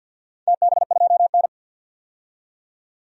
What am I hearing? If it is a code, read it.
Morse code, heard as a keyed tone, about 50 wpm, 700 Hz. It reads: TB2N
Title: DXCC-100 Call Signs - 4 Characters - 50wpm